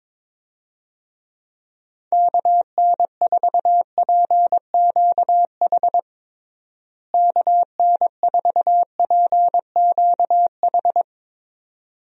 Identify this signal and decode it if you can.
Morse code, heard as a keyed tone, about 22 wpm, 700 Hz. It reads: KN4PQH KN4PQH
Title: USA Call Signs - 6 Characters - Repeated 2x - 22wpm